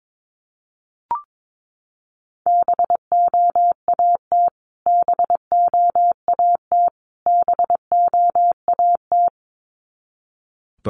Morse code, heard as a keyed tone, about 22 wpm, 700 Hz. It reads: BOAT BOAT BOAT
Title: Top 500 Words Repeated 22wpm